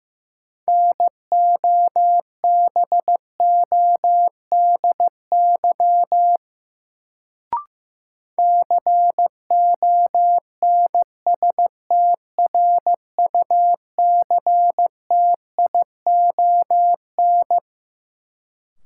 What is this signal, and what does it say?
Morse code, heard as a keyed tone, about 15 wpm, 700 Hz. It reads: NOBODY CONSTRUCTION
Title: Sentences from Top 1600 Words - New Word Drill - 15wpm